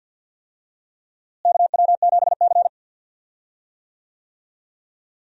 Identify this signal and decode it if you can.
Morse code, heard as a keyed tone, about 50 wpm, 700 Hz. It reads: KW7C
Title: USA Call Signs - 4 Characters - Repeated 2x - 50wpm